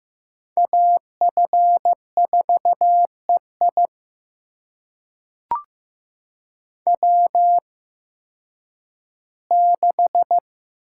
Morse code, heard as a keyed tone, about 15 wpm, spaced as for 10 wpm, 700 Hz. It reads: AF4EI W6
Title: USA Call Signs - 5 Characters - 8x Character Spacing - 15wpm